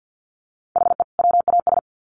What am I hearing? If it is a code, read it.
Morse code, heard as a keyed tone, about 50 wpm, 700 Hz. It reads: HE WAS
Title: Sets of 2 Words 50wpm